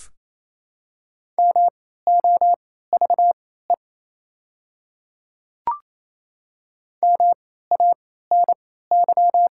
Morse code, heard as a keyed tone, about 28 wpm, spaced as for 16 wpm, 700 Hz. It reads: MOVE MANY
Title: 3x Character Spacing - Top 200 Words - 28wpm